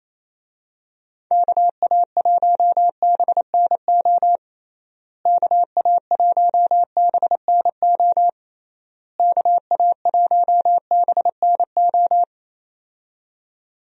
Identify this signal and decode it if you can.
Morse code, heard as a keyed tone, about 27 wpm, 700 Hz. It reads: KA1BNO KA1BNO KA1BNO
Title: USA Call Signs - 6 Characters - Repeated 3x - 28wpm